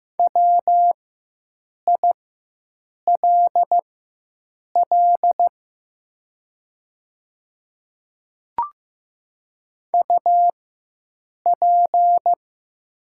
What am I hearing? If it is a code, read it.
Morse code, heard as a keyed tone, about 15 wpm, spaced as for 7 wpm, 700 Hz. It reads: WILL UP
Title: 4x Character Spacing - Top 200 Words - 15wpm